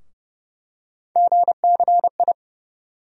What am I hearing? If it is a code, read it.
Morse code, heard as a keyed tone, about 29 wpm, 700 Hz. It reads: GCI
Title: USA Call Sign Postfixes - 3 Letters 30wpm